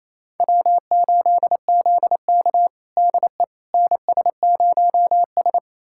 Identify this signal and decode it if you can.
Morse code, heard as a keyed tone, about 27 wpm, 700 Hz. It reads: W8ZK DE NS0S
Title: USA Callsign DE Callsign - 4 Characters - 28wpm